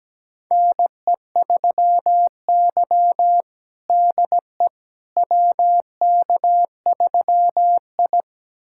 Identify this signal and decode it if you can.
Morse code, heard as a keyed tone, about 17 wpm, 700 Hz. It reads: NE3Y DE WK3I
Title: USA Callsign DE Callsign - 4 Characters - 17wpm